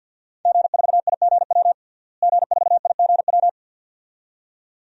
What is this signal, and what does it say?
Morse code, heard as a keyed tone, about 51 wpm, 700 Hz. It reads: G4IGW G4IGW
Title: DXCC-100 Call Signs - 5 Characters - Repeated 2x - 50wpm